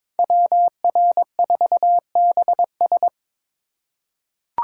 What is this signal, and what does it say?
Morse code, heard as a keyed tone, about 22 wpm, 700 Hz. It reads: WR4BS
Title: USA Call Signs - 5 Characters - 4x Character Spacing - 22wpm